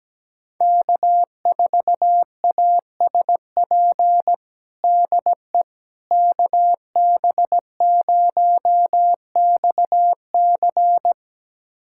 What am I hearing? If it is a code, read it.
Morse code, heard as a keyed tone, about 17 wpm, 700 Hz. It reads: K4ASP DE KB0XC
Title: USA Callsign DE Callsign - 5 Characters - 17wpm